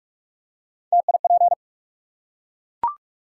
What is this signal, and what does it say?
Morse code, heard as a keyed tone, about 44 wpm, 700 Hz. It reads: TIP
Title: Three Letter Words - 45wpm